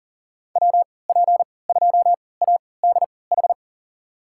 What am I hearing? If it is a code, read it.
Morse code, heard as a keyed tone, about 39 wpm, spaced as for 22 wpm, 700 Hz. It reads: WP2ADH
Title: USA Call Signs - 6 Characters - 3x Character Spacing - 40wpm